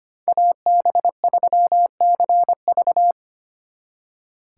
Morse code, heard as a keyed tone, about 32 wpm, 700 Hz. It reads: AB3CV